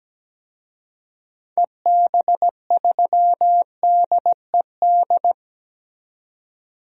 Morse code, heard as a keyed tone, about 17 wpm, 700 Hz. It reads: EB3DED
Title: DXCC-100 Call Signs - 6 Characters - 17wpm